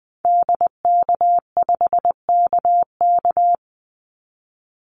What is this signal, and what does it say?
Morse code, heard as a keyed tone, about 20 wpm, 700 Hz. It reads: DK5KK
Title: Call Signs Repeated 2x - 20wpm